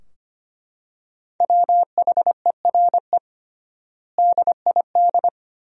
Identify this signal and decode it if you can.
Morse code, heard as a keyed tone, about 25 wpm, 700 Hz. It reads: WHERE DID
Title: 3x Word-Spacing - Sentences from Top 200 Words - 25wpm